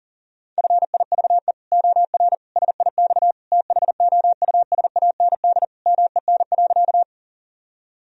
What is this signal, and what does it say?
Morse code, heard as a keyed tone, about 40 wpm, 700 Hz. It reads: FIVE OR SIX THOUSAND MEN.
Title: Sentences from Top 1000 Words - Review Part 3 - 40wpm